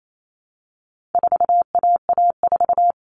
Morse code, heard as a keyed tone, about 28 wpm, 700 Hz. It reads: 4AA4
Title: All Cut Numbers - 4-Digit 28wpm